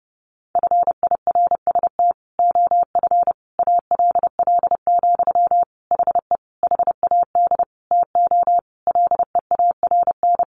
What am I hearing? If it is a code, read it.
Morse code, heard as a keyed tone, about 30 wpm, 700 Hz. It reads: FIRST OF ALL, HE HAD TO LEARN
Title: Sentences from Top 1200 Words 30wpm